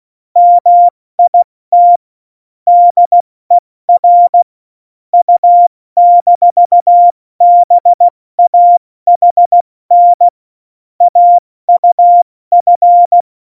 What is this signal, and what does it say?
Morse code, heard as a keyed tone, about 16 wpm, 700 Hz. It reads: MIT DER U-BAHN AUF
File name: Morsekurs / Witze / Witz 27. Oktober 2020, 70 F80